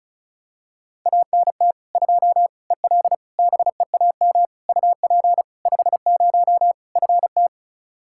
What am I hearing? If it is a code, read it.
Morse code, heard as a keyed tone, about 36 wpm, 700 Hz. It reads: ANT 2 EL BEAM UP 50 FT
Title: Practice QSO - Exchange 2 - Info - 35wpm